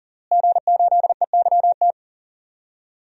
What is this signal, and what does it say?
Morse code, heard as a keyed tone, about 40 wpm, 700 Hz. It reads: G8EYT
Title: DXCC-100 Call Signs - 5 Characters - Repeated 3x - 40wpm